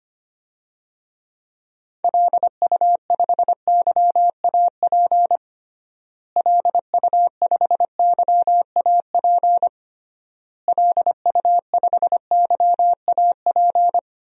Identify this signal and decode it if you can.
Morse code, heard as a keyed tone, about 25 wpm, 700 Hz. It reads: LU5YAP LU5YAP LU5YAP
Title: DXCC-100 Call Signs - 6 Characters - Repeated 3x - 25wpm